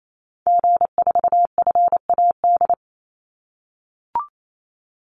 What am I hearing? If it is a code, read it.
Morse code, heard as a keyed tone, about 28 wpm, 700 Hz. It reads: G4FAD
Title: Call Signs - 28wpm